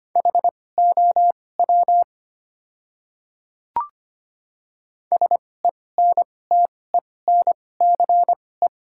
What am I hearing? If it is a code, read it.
Morse code, heard as a keyed tone, about 25 wpm, spaced as for 18 wpm, 700 Hz. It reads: HOW SENTENCE
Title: 2x Character Spacing - Top 300 Words - 25wpm